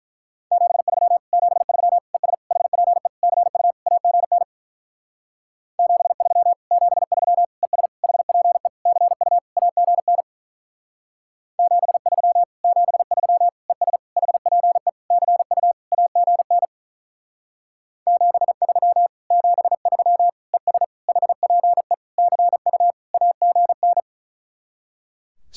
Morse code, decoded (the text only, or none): 73 73 ES HPE CU AGN 73 73 ES HPE CU AGN 73 73 ES HPE CU AGN 73 73 ES HPE CU AGN